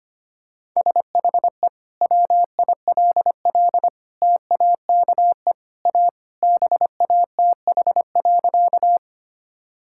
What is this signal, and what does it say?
Morse code, heard as a keyed tone, about 25 wpm, 700 Hz. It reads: SHE WILL TAKE A BATH.
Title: Sentences from Top 1600 Words 25wpm